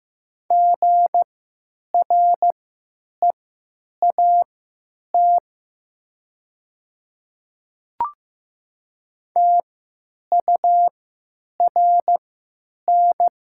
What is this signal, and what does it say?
Morse code, heard as a keyed tone, about 15 wpm, spaced as for 9 wpm, 700 Hz. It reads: GREAT TURN
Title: 3x Character Spacing - Top 300 Words - 15wpm